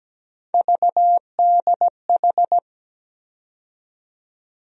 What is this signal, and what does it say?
Morse code, heard as a keyed tone, about 17 wpm, 700 Hz. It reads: VDH
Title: USA Call Sign Postfixes - 3 Letters 17wpm